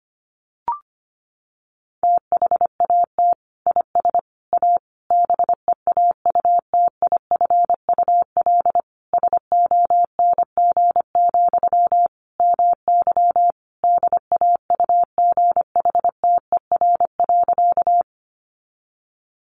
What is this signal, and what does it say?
Morse code, heard as a keyed tone, about 25 wpm, 700 Hz. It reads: THAT IS A BEAUTIFUL SONG, MY DAUGHTER.
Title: Sentences from Top 1500 Words 25wpm